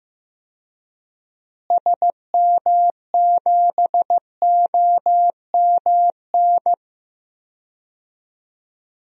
Morse code, heard as a keyed tone, about 15 wpm, 700 Hz. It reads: SM7OMN
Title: DXCC-100 Call Signs - 6 Characters - Repeated 3x - 15wpm